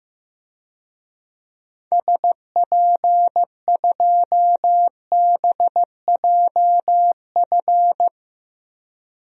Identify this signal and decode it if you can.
Morse code, heard as a keyed tone, about 15 wpm, 700 Hz. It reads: SP2BJF
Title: DXCC-100 Call Signs - 6 Characters - 15wpm